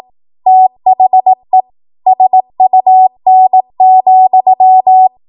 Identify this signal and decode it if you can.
Morse code, heard as a keyed tone, about 18 wpm, 750 Hz. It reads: THE SUN,
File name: Morse / 18WPM